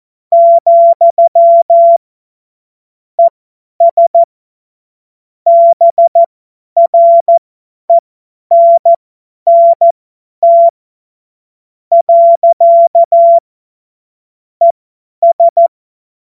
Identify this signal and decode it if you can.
Morse code, heard as a keyed tone, about 14 wpm, spaced as for 9 wpm, 680 Hz. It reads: , ES BRENNT. ES